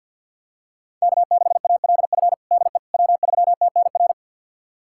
Morse code, heard as a keyed tone, about 52 wpm, 700 Hz. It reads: K6ALF DE W3TNR
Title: USA Callsign DE Callsign - 5 Characters - 50wpm